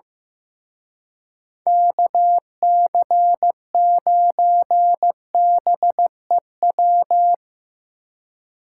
Morse code, heard as a keyed tone, about 17 wpm, 700 Hz. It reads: KC9BEW
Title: Speed Racing - USA 2x3 Call Signs 15wpm